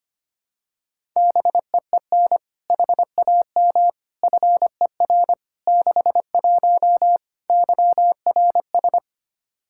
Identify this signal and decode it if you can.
Morse code, heard as a keyed tone, about 25 wpm, 700 Hz. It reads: BEEN HAM FER 61 YRS
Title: K1USN SST QSO Practice 25wpm